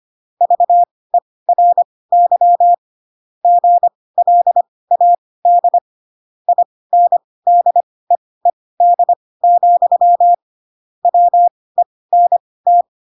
Morse code, heard as a keyed tone, about 25 wpm, spaced as for 18 wpm, 700 Hz. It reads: VERY GLAD INDEED, WENT
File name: Pinocchio / Pinocchio ch2 12 25 ik1hge